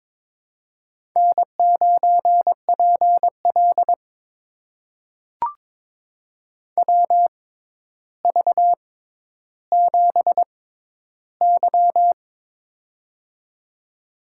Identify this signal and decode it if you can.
Morse code, heard as a keyed tone, about 23 wpm, spaced as for 11 wpm, 700 Hz. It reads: N9PL WV7Y
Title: USA Call Signs - 4 Characters - 6x Character Spacing - 22wpm